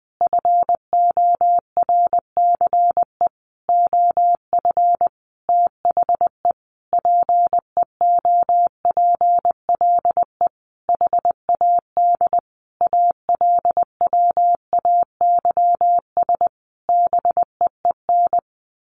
Morse code, heard as a keyed tone, about 20 wpm, 700 Hz. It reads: FORCE OF THE PEOPLE HAD ALWAYS BEEN
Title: Sentences from Top 1400 Words 20wpm